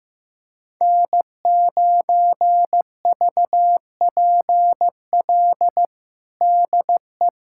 Morse code, heard as a keyed tone, about 16 wpm, 700 Hz. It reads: N9VPL DE
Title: USA Callsign DE Callsign - 5 Characters - 15wpm